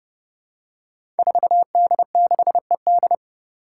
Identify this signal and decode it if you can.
Morse code, heard as a keyed tone, about 30 wpm, 700 Hz. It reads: 4D6ED
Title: Five Challenging Letter-Number Combinations 30wpm